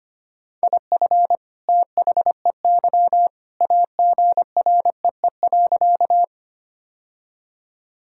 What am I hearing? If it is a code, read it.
Morse code, heard as a keyed tone, about 25 wpm, 700 Hz. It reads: IF THEY AGREE.
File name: practice / 25wpm / Sentences with 2-Word Phrases - 25wpm